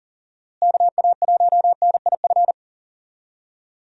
Morse code, heard as a keyed tone, about 40 wpm, 700 Hz. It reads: KA1NIF
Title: USA Call Signs - 6 Characters - Repeated 3x - 40wpm